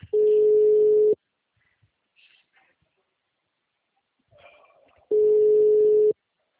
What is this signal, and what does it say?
Ringback tone (Europe, 425 Hz).
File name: out- -2006- - - .90049